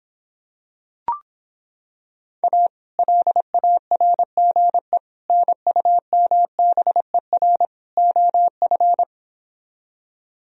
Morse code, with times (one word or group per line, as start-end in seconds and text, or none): 2.4-2.7 A
3.0-5.0 LARGE
5.3-7.7 NUMBER
8.0-9.0 OF